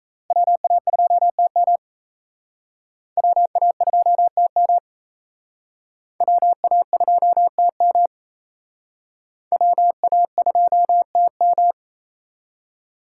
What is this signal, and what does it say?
Morse code, heard as a keyed tone, about 36 wpm, 700 Hz. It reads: WA2TM WA2TM WA2TM WA2TM